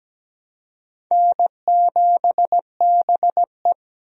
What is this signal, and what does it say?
Morse code, heard as a keyed tone, about 17 wpm, 700 Hz. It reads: N7BE
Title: USA Call Signs - 4 Characters - 17wpm